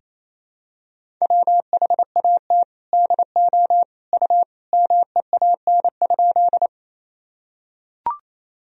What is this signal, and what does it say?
Morse code, heard as a keyed tone, about 28 wpm, 700 Hz. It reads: WHAT DO U MEAN?
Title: Sentences from Top 200 Words and Abbreviations 28wpm